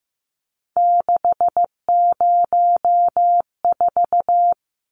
Morse code, heard as a keyed tone, about 15 wpm, 700 Hz. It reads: 604
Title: Numbers - 3-Digit 15wpm